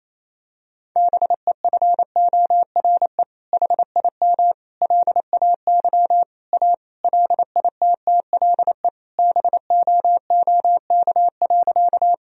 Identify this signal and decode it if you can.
Morse code, heard as a keyed tone, about 28 wpm, 700 Hz. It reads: BEFORE HIM LAY A LITTLE BOOK.
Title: Sentences from Top 1000 Words - Review Part 2 - 28wpm